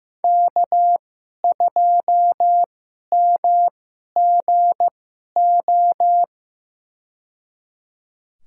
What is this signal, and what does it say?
Morse code, heard as a keyed tone, about 15 wpm, spaced as for 11 wpm, 700 Hz. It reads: K2MGO
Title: USA Call Signs - 5 Characters - 2x Character Spacing - 15wpm